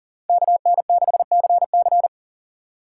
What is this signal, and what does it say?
Morse code, heard as a keyed tone, about 40 wpm, 700 Hz. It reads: KN6CC
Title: USA Call Signs - 5 Characters - 3x Character Spacing - 40wpm